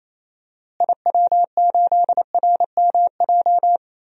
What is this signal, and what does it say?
Morse code, heard as a keyed tone, about 28 wpm, 700 Hz. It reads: IW8RMJ
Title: DXCC-100 Call Signs - 6 Characters - Repeated 2x - 28wpm